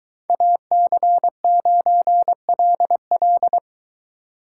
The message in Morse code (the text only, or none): AC9LL